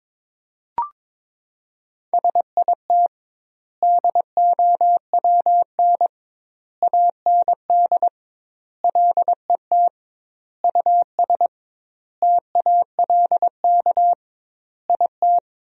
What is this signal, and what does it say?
Morse code, heard as a keyed tone, about 22 wpm, 700 Hz. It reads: SIT DOWN AND LET US TALK IT
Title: 2x Word-Spacing - Sentences from Top 500 Words - 22wpm